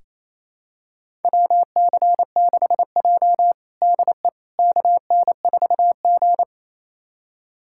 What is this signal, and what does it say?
Morse code, heard as a keyed tone, about 28 wpm, 700 Hz. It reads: WC6J DE KN4G
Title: USA Callsign DE Callsign - 4 Characters - 28wpm